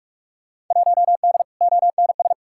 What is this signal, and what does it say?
Morse code, heard as a keyed tone, about 42 wpm, 700 Hz. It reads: 1D ONS
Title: Field Day Exchanges - 45wpm